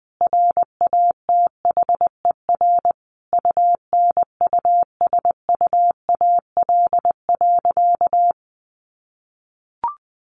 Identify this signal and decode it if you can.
Morse code, heard as a keyed tone, about 19 wpm, 700 Hz. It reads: RATHER UNUSUAL.
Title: Sentences from Top 1400 Words 20wpm